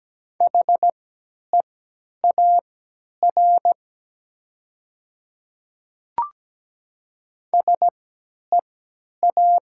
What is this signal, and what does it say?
Morse code, heard as a keyed tone, about 17 wpm, spaced as for 10 wpm, 700 Hz. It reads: HEAR SEA
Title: 3x Character Spacing - Top 300 Words - 17wpm